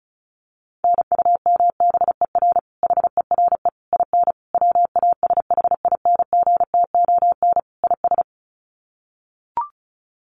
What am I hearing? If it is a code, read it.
Morse code, heard as a keyed tone, about 34 wpm, 700 Hz. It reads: NUMBER HERE IN WASHINGTON IS
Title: Sets of 5 Words Encore2 35wpm